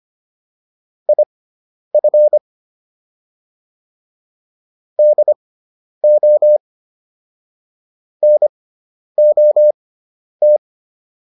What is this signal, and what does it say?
Morse code, heard as a keyed tone, about 25 wpm, spaced as for 10 wpm, 600 Hz. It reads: IF DO NOT
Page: intermediate-practice-files